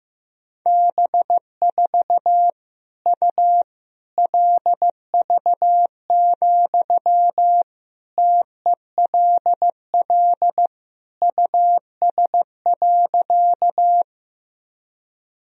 Morse code, heard as a keyed tone, about 15 wpm, 700 Hz. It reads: B4 U LV, TELL US.
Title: Sentences from Top 400 Words and Abbreviations 15wpm